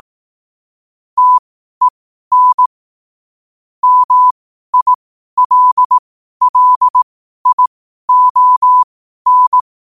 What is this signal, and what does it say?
Morse code, heard as a keyed tone, about 18 wpm, spaced as for 12 wpm, 1000 Hz. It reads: TEN MILLION
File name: qotdfiles / 12wpm